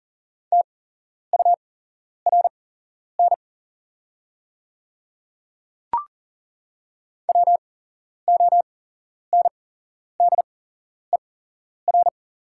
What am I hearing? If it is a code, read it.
Morse code, heard as a keyed tone, about 40 wpm, spaced as for 11 wpm, 690 Hz. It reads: TURN WONDER
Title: 8x Character Spacing - Top 500 Words - 40wpm